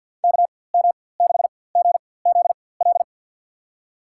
Morse code, heard as a keyed tone, about 53 wpm, spaced as for 24 wpm, 700 Hz. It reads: KM6GZL
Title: USA Call Signs - 6 Characters - 4x Character Spacing - 50wpm